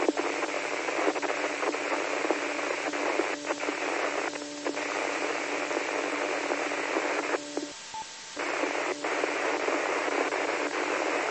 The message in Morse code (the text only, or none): none